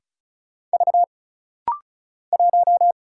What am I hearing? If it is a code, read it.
Morse code, heard as a keyed tone, about 36 wpm, 700 Hz. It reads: V 1
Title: Single Letter-Number Alt - Rapid-Fire 35wpm